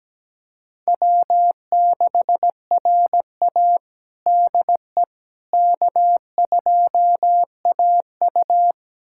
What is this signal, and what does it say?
Morse code, heard as a keyed tone, about 17 wpm, 700 Hz. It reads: W6RA DE K2AU
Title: USA Callsign DE Callsign - 4 Characters - 17wpm